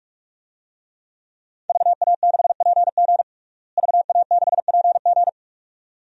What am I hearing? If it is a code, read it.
Morse code, heard as a keyed tone, about 47 wpm, 700 Hz. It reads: VA6PG VA6PG
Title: DXCC-100 Call Signs - 5 Characters - Repeated 2x - 45wpm